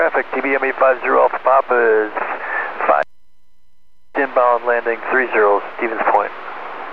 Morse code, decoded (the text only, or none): none